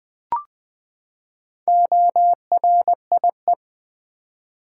Morse code, heard as a keyed tone, about 20 wpm, 700 Hz. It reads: ORIE
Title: Build A Word - 4 Letters At A Time - 11-Letter Words - 20wpm